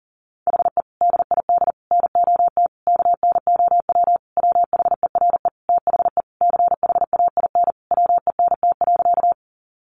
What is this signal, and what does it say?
Morse code, heard as a keyed tone, about 37 wpm, 700 Hz. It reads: HE DID NOT KNOW WHERE THE CHAIN WENT.
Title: Sentences from Top 1500 Words 40wpm